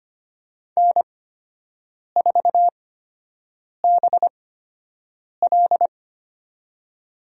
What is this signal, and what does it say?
Morse code, heard as a keyed tone, about 25 wpm, spaced as for 7 wpm, 700 Hz. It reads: N4BL
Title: USA Call Signs - 8x Character Spacing - 25wpm